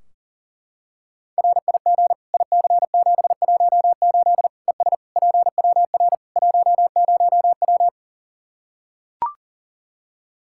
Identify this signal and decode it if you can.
Morse code, heard as a keyed tone, about 40 wpm, 700 Hz. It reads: RIG IC718 ES PWR 10W
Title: Practice QSO - Exchange 2 - Info - 40wpm